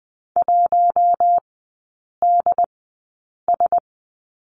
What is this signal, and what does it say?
Morse code, heard as a keyed tone, about 20 wpm, spaced as for 8 wpm, 700 Hz. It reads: 1DS
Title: 067 Lesson 12e - Letter d - Three Characters 20wpm